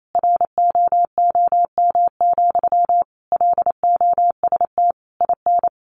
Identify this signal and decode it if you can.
Morse code, heard as a keyed tone, about 28 wpm, 700 Hz. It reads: ROOM, LOST IN
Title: Sentences from Top 1400 Words 28wpm